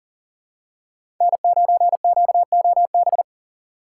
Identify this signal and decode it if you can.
Morse code, heard as a keyed tone, about 40 wpm, 700 Hz. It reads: N9QOB
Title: USA Call Signs - 5 Characters - 2x Character Spacing - 40wpm